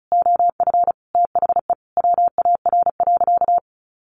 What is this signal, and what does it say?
Morse code, heard as a keyed tone, about 34 wpm, 700 Hz. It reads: OF THE WAR.
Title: Sentences from Top 1500 Words 35wpm